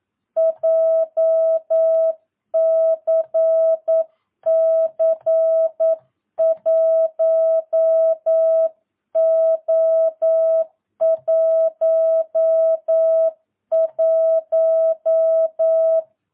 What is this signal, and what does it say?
Morse code, heard as a keyed tone, about 9 wpm, 650 Hz. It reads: JCC 1O11
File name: jh8cbh / 2022cw